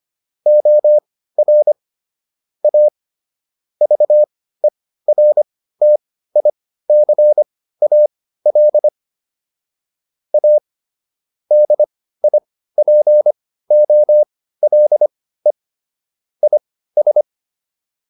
Morse code, heard as a keyed tone, about 25 wpm, spaced as for 15 wpm, 600 Hz. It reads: OR A VERTICAL A DIPOLE IS